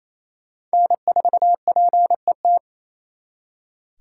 Morse code, heard as a keyed tone, about 34 wpm, 700 Hz. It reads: N4PET
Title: Speed Racing - USA 1x3 Call Signs 28wpm